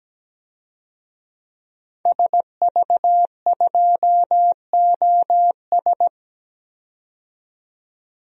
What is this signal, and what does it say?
Morse code, heard as a keyed tone, about 17 wpm, 700 Hz. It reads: SV2OS
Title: DXCC-100 Call Signs - 5 Characters - 17wpm